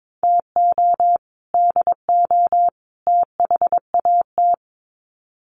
Morse code, heard as a keyed tone, about 22 wpm, 700 Hz. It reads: TO DO THAT
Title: Sets of 3 Words 22wpm